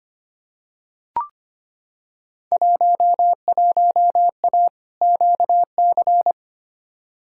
Morse code, heard as a keyed tone, about 25 wpm, 700 Hz. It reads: 11A QC
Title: Field Day Exchanges Encore - 25wpm